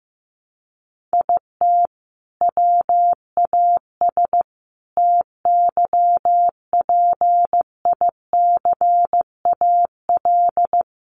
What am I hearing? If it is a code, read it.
Morse code, heard as a keyed tone, about 15 wpm, 700 Hz. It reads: IT WAS TYPICAL
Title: Sentences from Top 1100 Words 15wpm